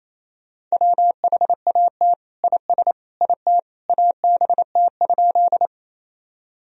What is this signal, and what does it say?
Morse code, heard as a keyed tone, about 28 wpm, 700 Hz. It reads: WHAT IS IT ABT?
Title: Sentences from Top 100 Words and Abbreviations 28wpm